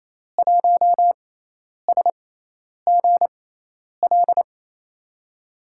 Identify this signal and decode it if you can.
Morse code, heard as a keyed tone, about 28 wpm, spaced as for 10 wpm, 700 Hz. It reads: 1SGL
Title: USA Call Signs - 5 Characters - 6x Character Spacing - 28wpm